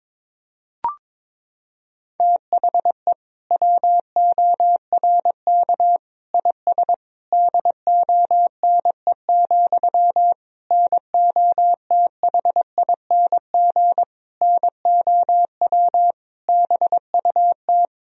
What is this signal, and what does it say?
Morse code, heard as a keyed tone, about 21 wpm, 700 Hz. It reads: THE WORK IS DONE, NOTHING NOW BUT
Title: Sentences from Top 1000 Words - Review Part 4 - 22wpm